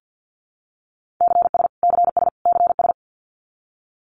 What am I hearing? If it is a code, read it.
Morse code, heard as a keyed tone, about 50 wpm, 700 Hz. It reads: CS CS CS
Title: Abbreviations Repeated 3x - 50wpm